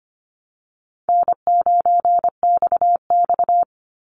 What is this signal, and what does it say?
Morse code, heard as a keyed tone, about 25 wpm, 700 Hz. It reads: N9XX
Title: Easy Call Signs Repeated 2x - 25wpm